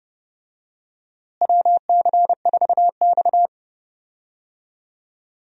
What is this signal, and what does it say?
Morse code, heard as a keyed tone, about 30 wpm, 700 Hz. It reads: WC4X
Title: USA Call Signs - 4 Characters - Repeated 2x - 30wpm